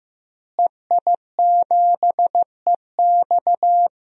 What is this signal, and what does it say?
Morse code, heard as a keyed tone, about 15 wpm, 700 Hz. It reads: EI7EX
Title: DXCC-100 Call Signs - 5 Characters - Repeated 3x - 15wpm